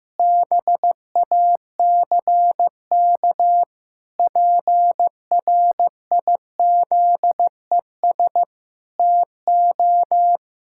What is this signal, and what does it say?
Morse code, heard as a keyed tone, about 16 wpm, 700 Hz. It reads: BACK PRIZES TO
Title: Pangram Challenge v2 - 15wpm